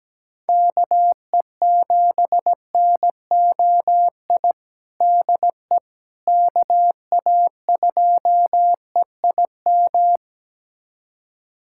Morse code, heard as a keyed tone, about 17 wpm, 700 Hz. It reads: KE7NOI DE KA2EIM
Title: USA Callsign DE Callsign - 6 Characters - 17wpm